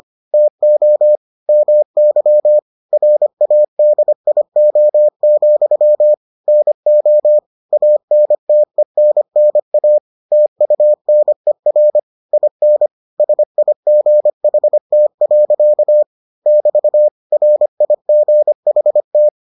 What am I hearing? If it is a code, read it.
Morse code, heard as a keyed tone, about 25 wpm, 600 Hz. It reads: TO MY RADIO, NO ANTENNA TUNER IN SIGHT. <BT> RIGHT